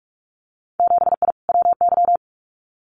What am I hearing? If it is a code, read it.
Morse code, heard as a keyed tone, about 46 wpm, 700 Hz. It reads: 7I WY
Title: Winter Field Day Exchanges Encore 45wpm